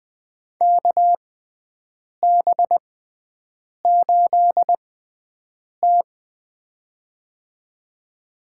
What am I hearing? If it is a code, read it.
Morse code, heard as a keyed tone, about 20 wpm, spaced as for 7 wpm, 700 Hz. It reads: KB8T
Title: USA Call Signs - 4 Characters - 6x Character Spacing - 20wpm